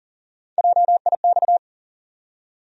Morse code, heard as a keyed tone, about 40 wpm, 700 Hz. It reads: JIX
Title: USA Call Sign Postfixes - 3 Letters 40wpm